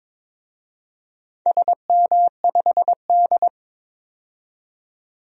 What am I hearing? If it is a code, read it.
Morse code, heard as a keyed tone, about 22 wpm, 700 Hz. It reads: SM5D
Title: DXCC-100 Call Signs - 4 Characters - 22wpm